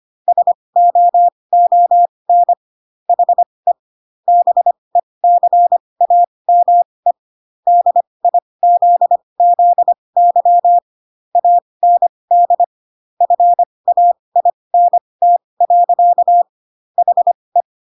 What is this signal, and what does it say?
Morse code, heard as a keyed tone, about 25 wpm, spaced as for 20 wpm, 700 Hz. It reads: SOON HE BECAME DIZZY AND FAINT. HE